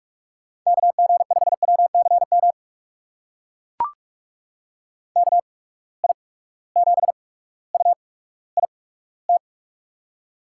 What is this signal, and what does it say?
Morse code, heard as a keyed tone, about 42 wpm, spaced as for 19 wpm, 700 Hz. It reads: KG5WCM KI7UIT
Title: USA Call Signs - 6 Characters - 8x Character Spacing - 45wpm